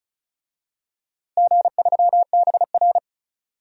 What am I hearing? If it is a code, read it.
Morse code, heard as a keyed tone, about 36 wpm, 700 Hz. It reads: G3BR
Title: DXCC-100 Call Signs - 4 Characters - Repeated 3x - 35wpm